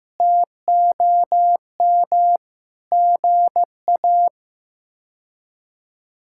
Morse code, heard as a keyed tone, about 15 wpm, 700 Hz. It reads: TOM GA
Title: K1USN SST QSO Practice - 15wpm